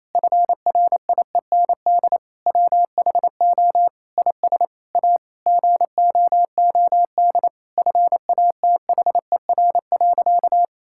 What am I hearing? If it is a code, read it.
Morse code, heard as a keyed tone, about 28 wpm, 700 Hz. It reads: FRIEND WHO IS A GOOD FATHER.
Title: Sentences from Top 1000 Words - Review Part 5 - 28wpm